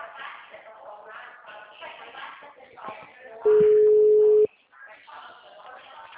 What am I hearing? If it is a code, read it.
Ringback tone (Europe, 425 Hz), with speech.
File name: out- -2042- - - .16511